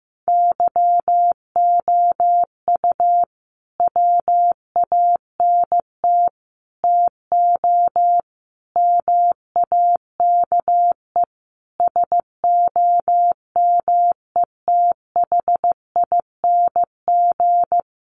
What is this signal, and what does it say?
Morse code, heard as a keyed tone, about 15 wpm, 700 Hz. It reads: YOU WANT TO MAKE SOMETHING
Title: Sentences from Top 600 Words 15wpm